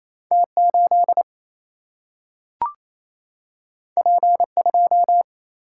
Morse code, heard as a keyed tone, about 28 wpm, 700 Hz. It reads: T8 P2
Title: Single-Letter Single-Number 28wpm